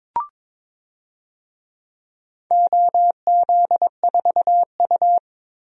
Morse code, heard as a keyed tone, about 22 wpm, 700 Hz. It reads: OZ4U
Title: DXCC-100 Call Signs - 4 Characters - 22wpm